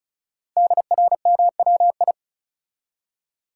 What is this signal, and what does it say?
Morse code, heard as a keyed tone, about 36 wpm, 700 Hz. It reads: DRMWI
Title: Code Groups - 5 Letters - 35wpm